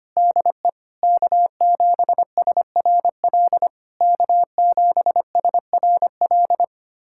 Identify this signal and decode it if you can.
Morse code, heard as a keyed tone, about 25 wpm, 690 Hz. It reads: DE K7SRL K7SRL
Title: SOTA QSO Practice 25wpm